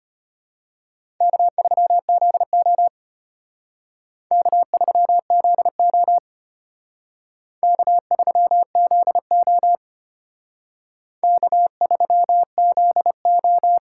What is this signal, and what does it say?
Morse code, heard as a keyed tone, about 31 wpm, 700 Hz. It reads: K3ZO K3ZO K3ZO K3ZO
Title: Speed Racing - USA 1x2 Call Signs 25wpm